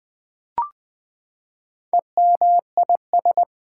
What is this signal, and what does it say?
Morse code, heard as a keyed tone, about 20 wpm, 700 Hz. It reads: EMIS
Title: Build A Word - 4 Letters At A Time - 8-Letter Words - 20wpm